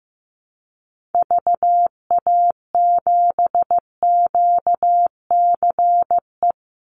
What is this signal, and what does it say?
Morse code, heard as a keyed tone, about 15 wpm, 700 Hz. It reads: VA7QCE
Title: Call Signs Repeated 3x - 15wpm